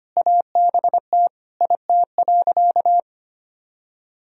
Morse code, heard as a keyed tone, about 25 wpm, 700 Hz. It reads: ABT IT.
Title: Sentences from Top 400 Words and Abbreviations 25wpm